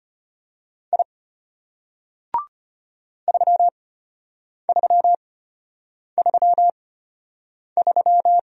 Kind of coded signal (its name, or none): Morse code